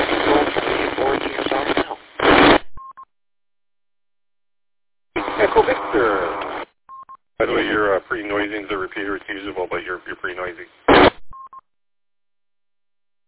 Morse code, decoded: none

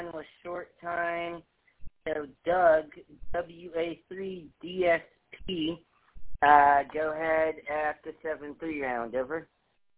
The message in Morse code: none

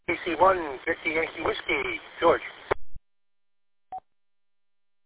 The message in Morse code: none

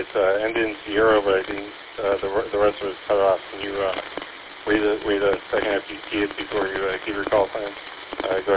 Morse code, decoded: none

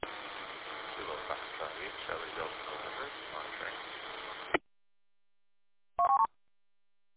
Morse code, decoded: none

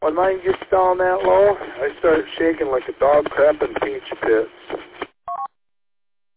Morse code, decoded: none

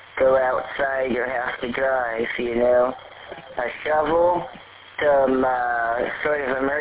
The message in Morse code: none

